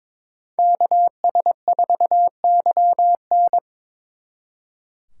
Morse code, heard as a keyed tone, about 25 wpm, 700 Hz. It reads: KS4YN
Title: Speed Racing - USA 2x2 Call Signs 22wpm